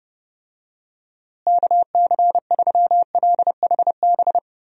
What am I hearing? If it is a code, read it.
Morse code, heard as a keyed tone, about 30 wpm, 700 Hz. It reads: KC3LHB
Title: USA Call Signs - 6 Characters - Repeated 3x - 30wpm